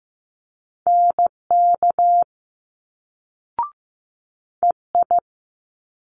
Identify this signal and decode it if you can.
Morse code, heard as a keyed tone, about 15 wpm, 700 Hz. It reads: NK EI
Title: Multi-letter Phonemes Quick Fire 15wpm